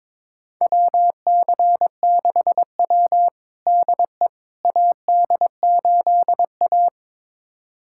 Morse code, heard as a keyed tone, about 22 wpm, 700 Hz. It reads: WC6W DE AD8A
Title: USA Callsign DE Callsign - 4 Characters - 22wpm